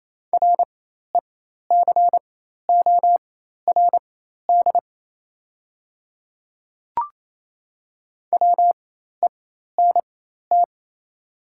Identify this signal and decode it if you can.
Morse code, heard as a keyed tone, about 28 wpm, spaced as for 13 wpm, 700 Hz. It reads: RECORD WENT
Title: 4x Character Spacing - Top 500 Words - 28wpm